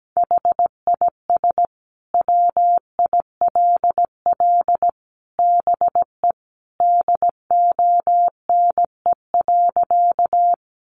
Morse code, heard as a keyed tone, about 17 wpm, 700 Hz. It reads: HIS WILL BE DONE.
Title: Sentences from Top 500 Words 17wpm